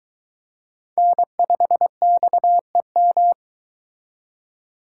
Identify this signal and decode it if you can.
Morse code, heard as a keyed tone, about 22 wpm, 700 Hz. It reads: N5XEM